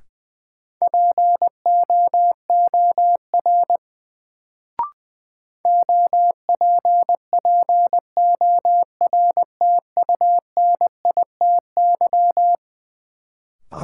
Morse code, heard as a keyed tone, about 21 wpm, 700 Hz. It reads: POOR OPPORTUNITY
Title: Sentences from Top 1000 Words - New Word Drill - 20wpm